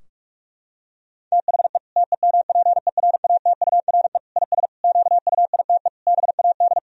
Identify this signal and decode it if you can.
Morse code, heard as a keyed tone, about 46 wpm, 700 Hz. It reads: THE TEMPERATURE IS QUITE BAD
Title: Sentences from Top 1000 Words - Review Part 4 - 45wpm